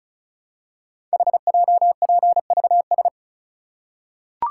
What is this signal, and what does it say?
Morse code, heard as a keyed tone, about 36 wpm, 700 Hz. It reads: HJPVS
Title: Code Groups - 5 Letters - 35wpm